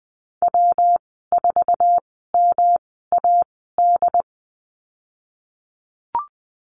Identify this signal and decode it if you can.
Morse code, heard as a keyed tone, about 20 wpm, spaced as for 15 wpm, 700 Hz. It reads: W4MAD
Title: 173 Lesson 28g - Number 9 - Call Signs 20wpm